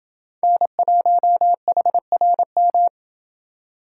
Morse code, heard as a keyed tone, about 30 wpm, 700 Hz. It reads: N1HRM